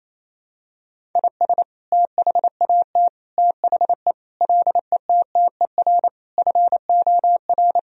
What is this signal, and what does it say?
Morse code, heard as a keyed tone, about 28 wpm, 700 Hz. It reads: IS THAT THE LETTER FOR
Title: Sentences from Top 1000 Words - Review Part 2 - 28wpm